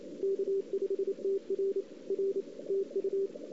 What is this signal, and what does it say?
Morse code, heard as a keyed tone, about 29 wpm, 410 Hz. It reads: K5TR R TU
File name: CW / 2008-ss-cw-k5tr-op-n5rz -02:00